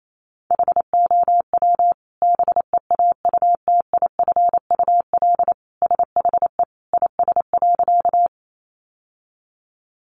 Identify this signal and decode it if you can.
Morse code, heard as a keyed tone, about 28 wpm, 700 Hz. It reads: HOW BEAUTIFUL SHE IS.
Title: Sentences from Top 900 Words 28wpm